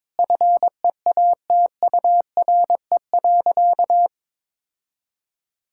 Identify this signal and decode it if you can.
Morse code, heard as a keyed tone, about 22 wpm, 700 Hz. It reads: FEATURE.
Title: Sentences from Top 1000 Words - Review Part 5 - 22wpm